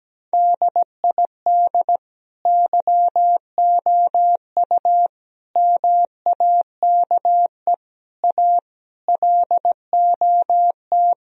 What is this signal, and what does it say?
Morse code, heard as a keyed tone, about 17 wpm, 700 Hz. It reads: DID YOU MAKE A LOT
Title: Sentences from Top 1000 Words - Review Part 4 - 17wpm